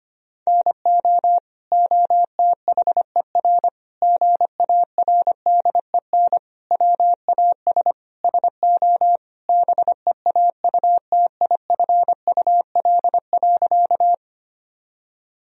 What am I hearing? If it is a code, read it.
Morse code, heard as a keyed tone, about 25 wpm, 700 Hz. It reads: NO OTHER GARDEN WAS SO BEAUTIFUL.
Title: Sentences from Top 1000 Words - Review Part 4 - 25wpm